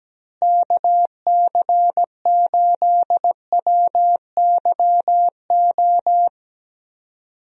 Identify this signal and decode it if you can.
Morse code, heard as a keyed tone, about 17 wpm, 700 Hz. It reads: KC8WYO
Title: USA Call Signs - 6 Characters - Repeated 2x - 17wpm